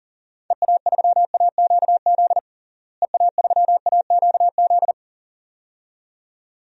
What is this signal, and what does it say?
Morse code, heard as a keyed tone, about 37 wpm, 700 Hz. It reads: EA3AQZ EA3AQZ
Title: DXCC-100 Call Signs - 6 Characters - Repeated 2x - 40wpm